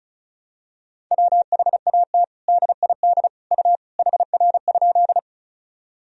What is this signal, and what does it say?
Morse code, heard as a keyed tone, about 36 wpm, 700 Hz. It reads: WHAT DID U HR?
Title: Sentences from Top 300 Words and Abbreviations 35wpm